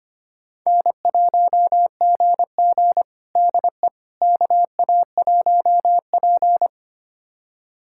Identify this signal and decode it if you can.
Morse code, heard as a keyed tone, about 25 wpm, 700 Hz. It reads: N1GG DE KA1P
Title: USA Callsign DE Callsign - 4 Characters - 25wpm